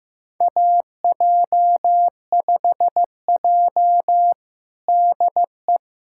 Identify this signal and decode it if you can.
Morse code, heard as a keyed tone, about 15 wpm, 700 Hz. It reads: AJ5J DE
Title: USA Callsign DE Callsign - 4 Characters - 15wpm